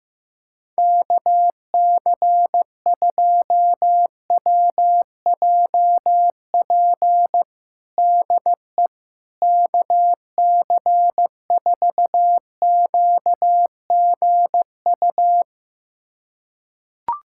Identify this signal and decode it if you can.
Morse code, heard as a keyed tone, about 16 wpm, 700 Hz. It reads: KC2WJP DE KC4QGU
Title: USA Callsign DE Callsign - 6 Characters - 15wpm